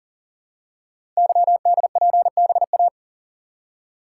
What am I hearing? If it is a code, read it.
Morse code, heard as a keyed tone, about 40 wpm, 690 Hz. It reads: YDPBA
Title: Code Groups - 5 Letters - 40wpm